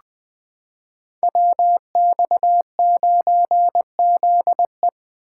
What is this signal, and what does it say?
Morse code, heard as a keyed tone, about 20 wpm, 700 Hz. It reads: WX9ZE
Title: Code Groups - 5 Letter:Numbers - 20wpm